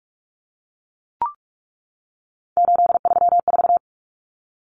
Morse code, heard as a keyed tone, about 44 wpm, 700 Hz. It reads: 834
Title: Common Cut Numbers - 3-Digit 45wpm